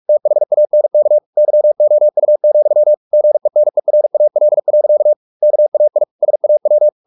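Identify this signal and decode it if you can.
Morse code, heard as a keyed tone, about 44 wpm, 600 Hz. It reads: THANK YOU, GENERAL. KAI SAW
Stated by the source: Morse practice at 45 wpm, the novel Never by Ken Follett in English, from CW med Gnister.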